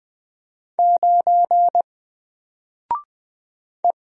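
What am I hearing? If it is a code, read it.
Morse code, heard as a keyed tone, about 20 wpm, 700 Hz. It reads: 9 E
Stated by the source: Single letter and number Morse drill at 20 wpm, from N4LCD.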